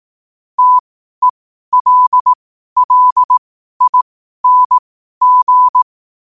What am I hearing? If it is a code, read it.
Morse code, heard as a keyed tone, about 18 wpm, spaced as for 12 wpm, 1000 Hz. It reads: TELLING
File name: qotdfiles / 12wpm